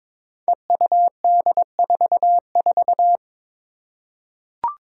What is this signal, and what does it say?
Morse code, heard as a keyed tone, about 21 wpm, 700 Hz. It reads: EUD44
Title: Five Challenging Letter-Number Combinations 22wpm